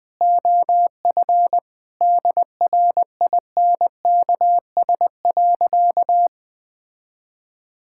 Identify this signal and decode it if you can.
Morse code, heard as a keyed tone, about 20 wpm, 700 Hz. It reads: OF DRINKS.
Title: Sentences with 6-Word Phrases - 20wpm